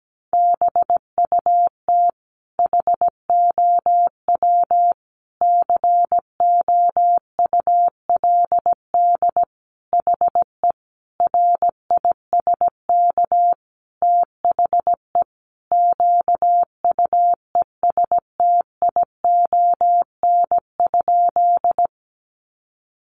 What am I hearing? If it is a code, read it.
Morse code, heard as a keyed tone, about 17 wpm, 700 Hz. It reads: BUT HOW COULD HE RISK THE QUESTION?
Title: Sentences from Top 600 Words 17wpm